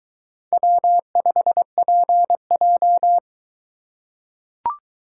Morse code, heard as a keyed tone, about 23 wpm, 700 Hz. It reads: W5PJ